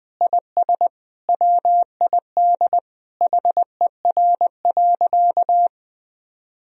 Morse code, heard as a keyed tone, about 20 wpm, 700 Hz. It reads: IS WID HER.
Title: Sentences from Top 100 Words and Abbreviations 20wpm